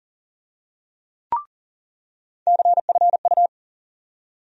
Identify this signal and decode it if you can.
Morse code, heard as a keyed tone, about 40 wpm, 700 Hz. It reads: CFU
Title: USA Call Sign Postfixes - 3 Letters 40wpm